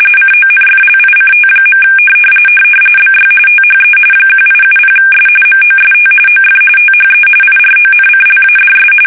RTTY, about 45.45 baud. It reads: CONFLICT. BY THE 1870S MOST OF THE IRONCLADS
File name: fsk850 / usnavy3